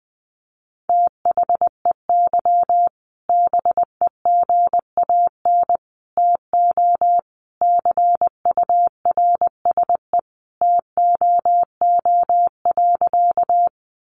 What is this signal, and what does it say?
Morse code, heard as a keyed tone, about 20 wpm, 700 Hz. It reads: THEY BEGAN TO CURSE TOO.
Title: A Sound Of Thunder 20wpm